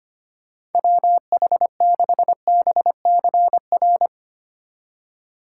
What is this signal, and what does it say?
Morse code, heard as a keyed tone, about 24 wpm, 700 Hz. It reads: WH6BCR
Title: Speed Racing - USA 2x3 Call Signs 25wpm